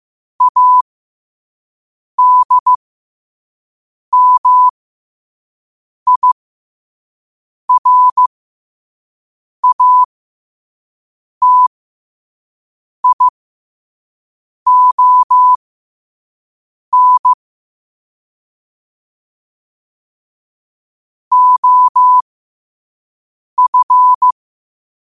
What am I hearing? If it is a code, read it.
Morse code, heard as a keyed tone, about 15 wpm, spaced as for 5 wpm, 1000 Hz. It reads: ADMIRATION OF
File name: qotdfiles / 5wpm